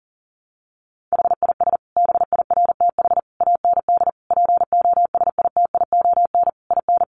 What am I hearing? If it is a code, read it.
Morse code, heard as a keyed tone, about 40 wpm, 700 Hz. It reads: HIS BIRTH AND POSITION IN
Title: Sentences from Top 900 Words 40wpm